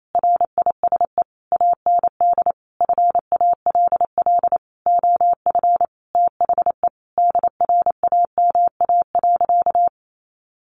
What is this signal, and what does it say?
Morse code, heard as a keyed tone, about 28 wpm, 700 Hz. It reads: RISE AND FALL OF THE DRAMA.
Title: Sentences from Top 1500 Words 28wpm